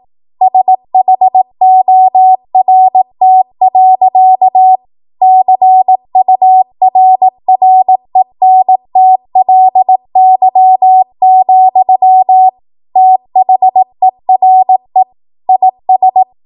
Morse code, heard as a keyed tone, about 18 wpm, 750 Hz. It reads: SHORT. CURRENTLY, THERE IS
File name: Morse / 18WPM